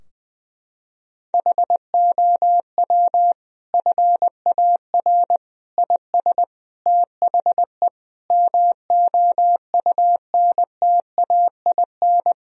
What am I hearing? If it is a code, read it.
Morse code, heard as a keyed tone, about 21 wpm, 700 Hz. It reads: HOW FAR IS THE MOUNTAIN
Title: Sentences from Top 1000 Words - Review Part 4 - 20wpm